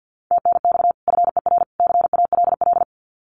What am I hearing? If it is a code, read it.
Morse code, heard as a keyed tone, about 50 wpm, 700 Hz. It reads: TNX FER CALL
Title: Common QSO Phrases 50wpm